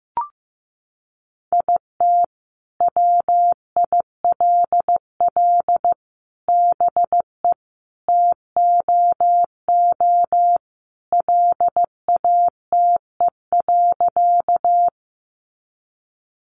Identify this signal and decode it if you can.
Morse code, heard as a keyed tone, about 15 wpm, 700 Hz. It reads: IT WILL BE TOO LATE.
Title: Sentences from Top 300 Words 15wpm